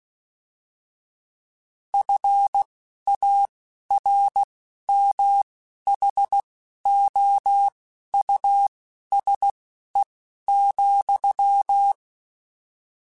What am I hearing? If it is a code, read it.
Morse code, heard as a keyed tone, about 16 wpm, spaced as for 11 wpm, 800 Hz. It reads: FARMHOUSE,
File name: General / General-mp3s / Houses @10